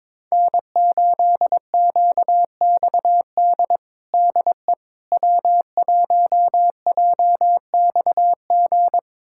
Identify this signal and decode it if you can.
Morse code, heard as a keyed tone, about 22 wpm, 700 Hz. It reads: N8QXD DE W1JXG
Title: USA Callsign DE Callsign 22wpm